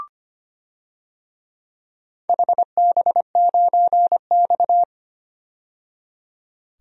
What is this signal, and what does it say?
Morse code, heard as a keyed tone, about 25 wpm, 700 Hz. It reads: HB9X
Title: DXCC-100 Call Signs - 4 Characters - 25wpm